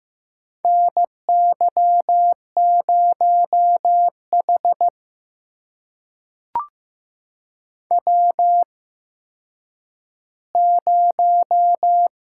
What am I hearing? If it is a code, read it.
Morse code, heard as a keyed tone, about 15 wpm, spaced as for 9 wpm, 700 Hz. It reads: NY0H W0
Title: USA Call Signs - 4 Characters - 8x Character Spacing - 15wpm